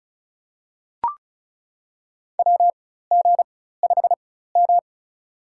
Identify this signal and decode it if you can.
Morse code, heard as a keyed tone, about 36 wpm, spaced as for 16 wpm, 700 Hz. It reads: WG5M
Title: USA Call Signs - 4 Characters - 4x Character Spacing - 35wpm